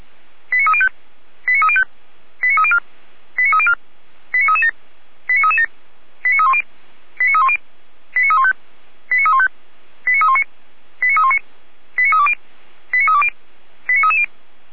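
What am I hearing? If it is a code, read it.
Five-tone selective call (ZVEI), heard as quick runs of five tones, tones of about 80 ms. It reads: 89305 89305 89304 89304 89337 89337 89310 89310 89315 89315 89319 89319 89320 89320 89330
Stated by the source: ZVEI five-tone selective call 89250, received on an SDR, from the Feuerwehr Orlamünde fire brigade audio server.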